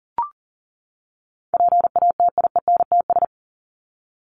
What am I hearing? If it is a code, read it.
Morse code, heard as a keyed tone, about 40 wpm, 700 Hz. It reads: PATIENTS
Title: ES:S Suffix - Top 100 Words with Base Words 40wpm